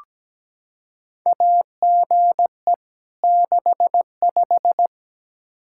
Morse code, heard as a keyed tone, about 17 wpm, 700 Hz. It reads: AGE 65
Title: K1USN SST QSO Practice 17wpm